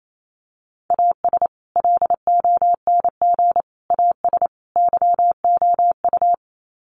Morse code, heard as a keyed tone, about 28 wpm, 700 Hz. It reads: AS LONG AS YOU
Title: Sets of 4 Words 28wpm